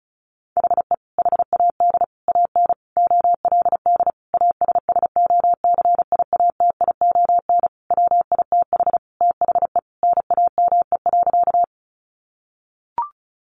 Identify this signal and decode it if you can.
Morse code, heard as a keyed tone, about 34 wpm, 700 Hz. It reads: HE HAD AN OLD ASSOCIATION WITH THE NAME.
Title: Sentences from Top 1100 Words 35wpm